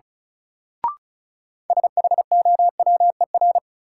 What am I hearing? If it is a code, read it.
Morse code, heard as a keyed tone, about 34 wpm, 700 Hz. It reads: SHOWER
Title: Sentences from Top 1600 Words - New Word Drill - 35wpm